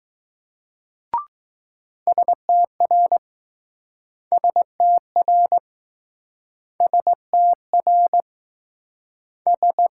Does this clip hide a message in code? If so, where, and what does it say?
Yes: Morse code from 2.1 to 9.9 s, heard as a keyed tone, about 20 wpm, 700 Hz. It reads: STR STR STR S